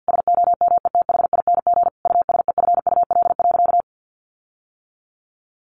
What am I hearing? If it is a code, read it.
Morse code, heard as a keyed tone, about 50 wpm, 700 Hz. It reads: SOMETHING USEFUL.
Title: Sentences from Top 1400 Words 50wpm